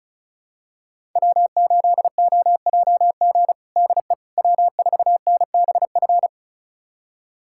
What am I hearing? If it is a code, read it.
Morse code, heard as a keyed tone, about 36 wpm, 700 Hz. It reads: W8OJG DE W4NBF
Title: USA Callsign DE Callsign - 5 Characters - 35wpm